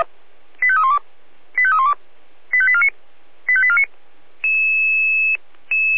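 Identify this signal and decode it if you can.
Five-tone selective call (ZVEI), heard as quick runs of five tones, tones of about 70 ms. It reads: 85312 85312 85849 85849
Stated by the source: ZVEI five-tone call received by SDR.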